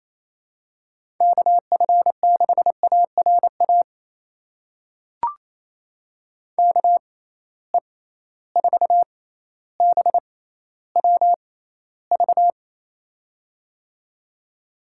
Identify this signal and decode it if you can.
Morse code, heard as a keyed tone, about 28 wpm, spaced as for 14 wpm, 700 Hz. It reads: KF6ARA KE4BWV
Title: USA Call Signs - 6 Characters - 6x Character Spacing - 28wpm